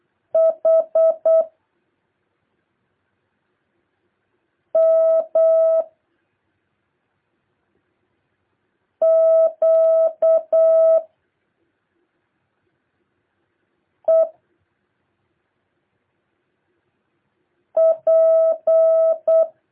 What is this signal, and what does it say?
Morse code, heard as a keyed tone, about 8 wpm, 650 Hz. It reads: H M Q E P
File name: jh8cbh / 2023cw